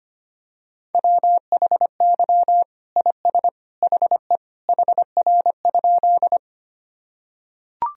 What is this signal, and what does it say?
Morse code, heard as a keyed tone, about 25 wpm, 700 Hz. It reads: WHY IS HE HR?
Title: Sentences from Top 200 Words and Abbreviations 25wpm